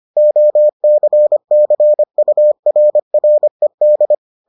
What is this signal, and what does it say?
Morse code, heard as a keyed tone, about 25 wpm, 600 Hz. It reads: OCCURRED